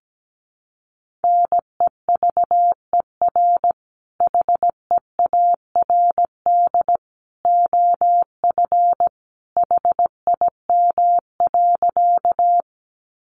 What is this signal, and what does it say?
Morse code, heard as a keyed tone, about 17 wpm, 700 Hz. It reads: NEVER HEARD OF HIM.
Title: Sentences from Top 400 Words 17wpm